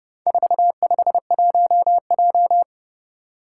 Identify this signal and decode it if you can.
Morse code, heard as a keyed tone, about 30 wpm, 700 Hz. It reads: 451J
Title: Four Challenging Letter-Number Combinations 30wpm